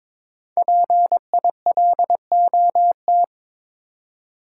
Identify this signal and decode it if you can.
Morse code, heard as a keyed tone, about 22 wpm, 700 Hz. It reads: PILOT
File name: 5 Letter Words - First 3 Letters Given - 22wpm